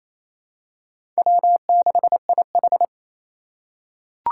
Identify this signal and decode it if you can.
Morse code, heard as a keyed tone, about 28 wpm, 700 Hz. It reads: W6IH
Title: USA Call Signs - 4 Characters - 4x Character Spacing - 28wpm